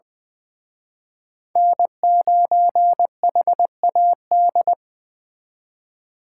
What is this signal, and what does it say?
Morse code, heard as a keyed tone, about 24 wpm, 700 Hz. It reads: N9HAD
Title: Speed Racing - USA 1x3 Call Signs 20wpm